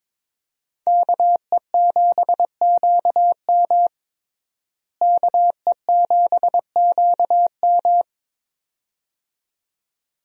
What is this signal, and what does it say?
Morse code, heard as a keyed tone, about 22 wpm, 700 Hz. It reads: KE7QM KE7QM
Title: USA Call Signs - 5 Characters - Repeated 2x - 22wpm